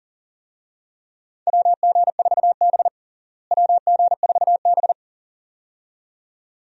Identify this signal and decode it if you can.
Morse code, heard as a keyed tone, about 40 wpm, 700 Hz. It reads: WG4B WG4B
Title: USA Call Signs - 4 Characters - Repeated 2x - 40wpm